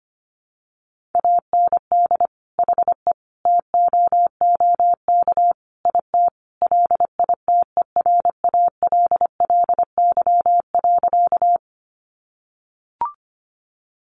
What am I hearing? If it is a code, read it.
Morse code, heard as a keyed tone, about 25 wpm, 700 Hz. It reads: AND HE TOOK IT LITERALLY.
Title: Sentences from Top 1400 Words 25wpm